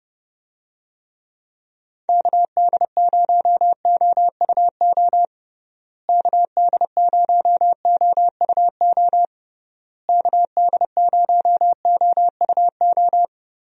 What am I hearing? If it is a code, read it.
Morse code, heard as a keyed tone, about 30 wpm, 700 Hz. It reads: KD0OUO KD0OUO KD0OUO
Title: USA Call Signs - 6 Characters - Repeated 3x - 30wpm